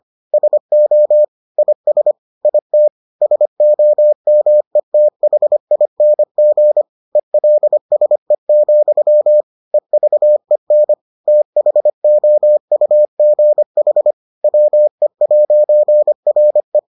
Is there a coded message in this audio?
Yes: Morse code from 0.3 to 16.8 s, heard as a keyed tone, about 25 wpm, 600 Hz. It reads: SO IS IT SOMETHING ELSE, EVEN THOUGH WE'RE